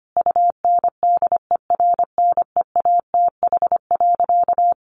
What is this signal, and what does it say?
Morse code, heard as a keyed tone, about 25 wpm, 700 Hz. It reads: UNDERNEATH.
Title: A Sound Of Thunder 25wpm